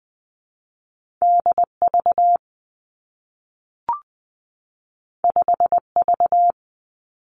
Morse code, heard as a keyed tone, about 20 wpm, 700 Hz. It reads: D V 5 V
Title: Two Challenging Letter-Number Combinations 20wpm